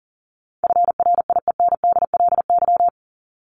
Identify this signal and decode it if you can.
Morse code, heard as a keyed tone, about 39 wpm, 700 Hz. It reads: FRIENDLY
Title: LY Suffix - Top 100 Words with Base Words 40wpm